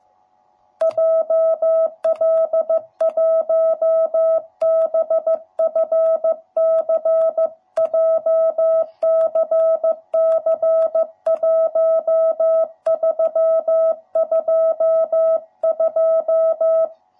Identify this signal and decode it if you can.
Morse code, heard as a keyed tone, about 14 wpm, 660 Hz. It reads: JL1BFC JCC1322
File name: jh8cbh / 2021cw / cw07b